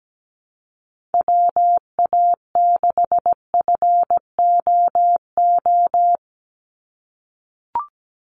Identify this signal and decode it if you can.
Morse code, heard as a keyed tone, about 17 wpm, 700 Hz. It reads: WA6FOO
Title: USA Call Signs 17wpm